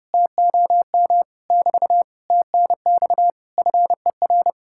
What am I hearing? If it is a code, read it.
Morse code, heard as a keyed tone, about 33 wpm, 700 Hz. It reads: TOM <BT> TNX FER